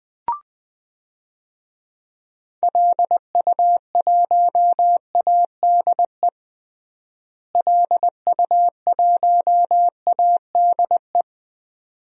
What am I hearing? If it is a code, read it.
Morse code, heard as a keyed tone, about 20 wpm, 700 Hz. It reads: LU1ADE LU1ADE
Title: DXCC-100 Call Signs - 6 Characters - Repeated 2x - 20wpm